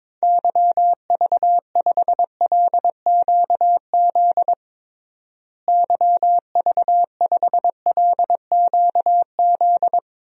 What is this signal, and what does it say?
Morse code, heard as a keyed tone, about 22 wpm, 700 Hz. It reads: YV5LQZ YV5LQZ
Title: DXCC-100 Call Signs - 6 Characters - Repeated 2x - 22wpm